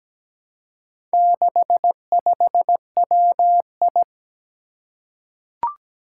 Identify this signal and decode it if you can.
Morse code, heard as a keyed tone, about 17 wpm, 700 Hz. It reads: 65WI
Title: Four Challenging Letter-Number Combinations 17wpm